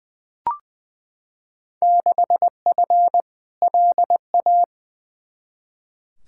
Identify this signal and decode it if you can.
Morse code, heard as a keyed tone, about 20 wpm, 700 Hz. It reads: 6F LA
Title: Field Day Exchanges Encore - 20wpm